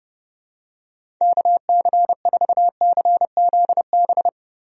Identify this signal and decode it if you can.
Morse code, heard as a keyed tone, about 30 wpm, 700 Hz. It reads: KC4CZB
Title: USA Call Signs - 6 Characters - Repeated 2x - 30wpm